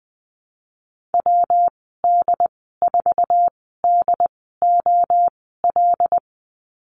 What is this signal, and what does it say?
Morse code, heard as a keyed tone, about 20 wpm, spaced as for 15 wpm, 700 Hz. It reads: WD4DOL
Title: 179 Lesson 29g - Slash - Call Signs 20wpm